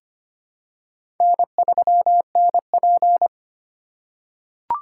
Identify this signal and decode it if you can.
Morse code, heard as a keyed tone, about 25 wpm, 700 Hz. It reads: N3NP
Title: USA Call Signs - 4 Characters - 4x Character Spacing - 25wpm